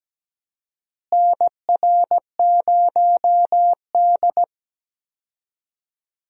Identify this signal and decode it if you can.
Morse code, heard as a keyed tone, about 17 wpm, 700 Hz. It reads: NR0D
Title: USA Call Signs - 4 Characters - Repeated 3x - 17wpm